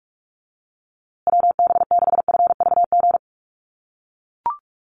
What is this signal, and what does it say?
Morse code, heard as a keyed tone, about 44 wpm, 700 Hz. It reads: WB6FVG
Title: USA Call Signs Repeated 3x 45wpm